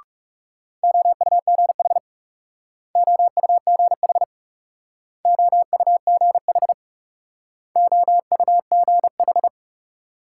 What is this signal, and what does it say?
Morse code, heard as a keyed tone, about 38 wpm, 700 Hz. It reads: OUGH OUGH OUGH OUGH